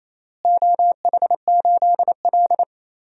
Morse code, heard as a keyed tone, about 28 wpm, 700 Hz. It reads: OH8L
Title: DXCC-100 Call Signs - 4 Characters - Repeated 3x - 28wpm